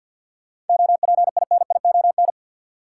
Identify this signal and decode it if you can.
Morse code, heard as a keyed tone, about 53 wpm, 700 Hz. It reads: OPINION
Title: Sentences from Top 1200 Words - New Word Drill Alt - 50wpm